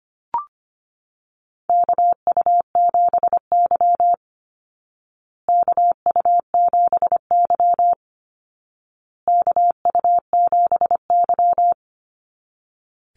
Morse code, heard as a keyed tone, about 25 wpm, 700 Hz. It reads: KU7Y KU7Y KU7Y
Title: Call Signs Repeated 2x - 25wpm